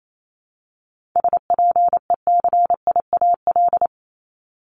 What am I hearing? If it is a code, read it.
Morse code, heard as a keyed tone, about 28 wpm, 700 Hz. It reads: SPECIAL
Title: Multi-letter Phonemes 28wpm